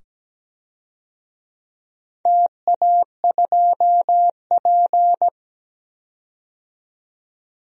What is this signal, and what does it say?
Morse code, heard as a keyed tone, about 17 wpm, 700 Hz. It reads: TA2P
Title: DXCC-100 Call Signs - 4 Characters - 17wpm